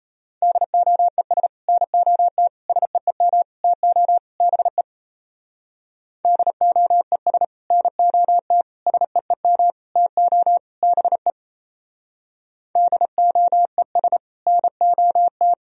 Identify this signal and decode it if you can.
Morse code, heard as a keyed tone, about 36 wpm, 700 Hz. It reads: DOES NOT SEEM TO BE DOES NOT SEEM TO BE DOES NOT